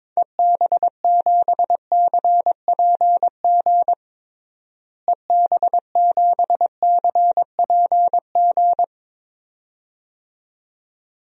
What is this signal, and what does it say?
Morse code, heard as a keyed tone, about 21 wpm, 700 Hz. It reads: EB7CPG EB7CPG
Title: DXCC-100 Call Signs - 6 Characters - Repeated 2x - 22wpm